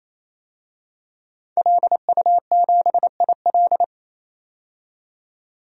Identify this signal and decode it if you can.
Morse code, heard as a keyed tone, about 28 wpm, 700 Hz. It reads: LU7IL
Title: DXCC-100 Call Signs - 5 Characters - 28wpm